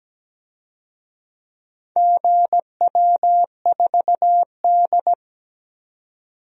Morse code, heard as a keyed tone, about 17 wpm, 700 Hz. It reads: GW4D
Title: DXCC-100 Call Signs - 4 Characters - 17wpm